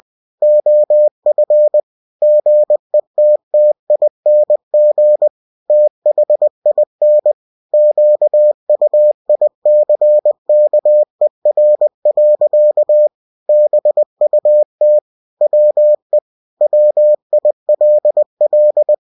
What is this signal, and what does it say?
Morse code, heard as a keyed tone, about 20 wpm, 600 Hz. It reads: OF GETTING THIN QUICKER. BUT WE WILL
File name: cw-books / winnie-the-pooh / winnie-20wpm-page-0019